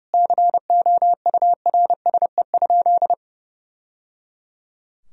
Morse code, heard as a keyed tone, about 30 wpm, 700 Hz. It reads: COURSE?
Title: Sentences from Top 1000 Words - Review Part 2 - 30wpm